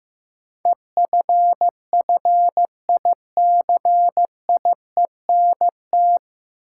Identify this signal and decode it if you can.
Morse code, heard as a keyed tone, about 15 wpm, 700 Hz. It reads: EFFICIENT
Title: Sentences from Top 1300 Words - New Word Drill Alt - 15wpm